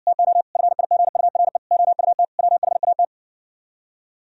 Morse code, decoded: TO FIGURE OUT WHAT

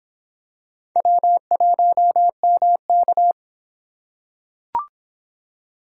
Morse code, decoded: W1MK

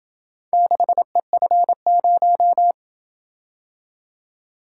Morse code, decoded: 6EF0

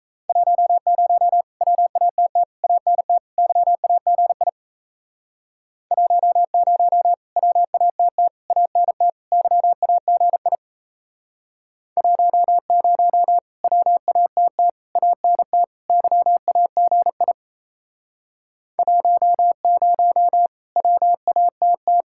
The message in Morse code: 10 WATT ANT YAGI 10 WATT ANT YAGI 10 WATT ANT YAGI 10 WATT

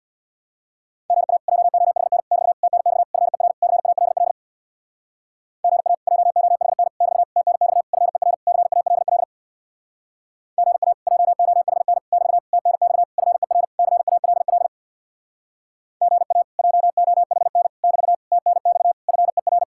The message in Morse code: GA JOHN <BT> TNX FER CALL GA JOHN <BT> TNX FER CALL GA JOHN <BT> TNX FER CALL GA JOHN <BT> TNX FER

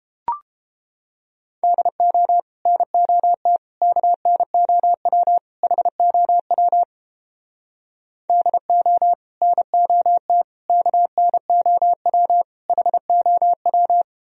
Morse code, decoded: DO NOT KNOW HOW DO NOT KNOW HOW

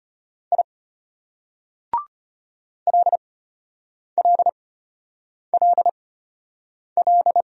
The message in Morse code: I LLLL

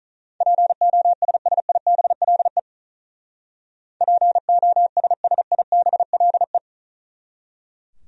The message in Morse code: POSSIBLE POSSIBLE

